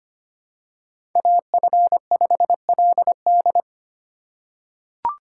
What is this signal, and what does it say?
Morse code, heard as a keyed tone, about 25 wpm, 700 Hz. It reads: AF5LD
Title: USA Call Signs - 5 Characters - 4x Character Spacing - 25wpm